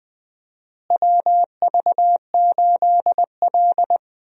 Morse code, heard as a keyed tone, about 20 wpm, 700 Hz. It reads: WV8L
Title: USA Call Signs - 4 Characters - Repeated 2x - 20wpm